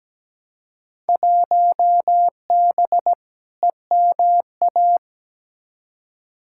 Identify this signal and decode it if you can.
Morse code, heard as a keyed tone, about 17 wpm, 700 Hz. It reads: 1B EMA
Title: Field Day Exchanges - 17wpm